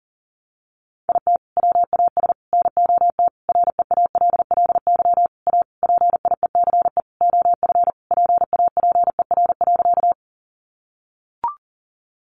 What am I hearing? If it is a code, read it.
Morse code, heard as a keyed tone, about 40 wpm, 700 Hz. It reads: IT WAS NOT REALLY A PIECE OF PAPER.
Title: Sentences from Top 600 Words 40wpm